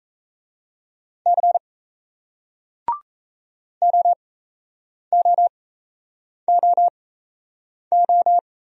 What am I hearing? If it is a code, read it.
Morse code, heard as a keyed tone, about 38 wpm, spaced as for 8 wpm, 700 Hz. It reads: C OOOO